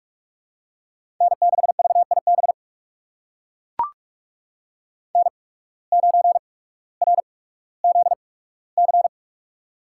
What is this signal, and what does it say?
Morse code, heard as a keyed tone, about 44 wpm, spaced as for 19 wpm, 700 Hz. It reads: N6VIB N9RZC
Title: USA Call Signs - 8x Character Spacing - 45wpm